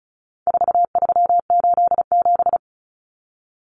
Morse code, heard as a keyed tone, about 34 wpm, 700 Hz. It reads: 4387
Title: Numbers - 4-Digit 35wpm